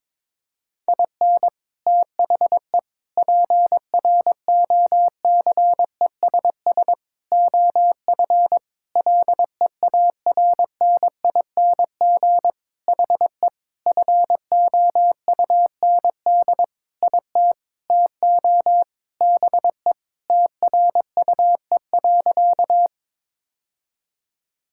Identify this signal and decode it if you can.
Morse code, heard as a keyed tone, about 22 wpm, 700 Hz. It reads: IN THE PROCESS OF LEARNING HE FOUND IT TO BE TRUE.
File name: practice / 22wpm / Sentences with 4-Word Phrases - 22wpm